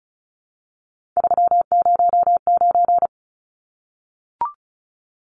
Morse code, heard as a keyed tone, about 34 wpm, 700 Hz. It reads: 309
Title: Numbers - 3-Digit 35wpm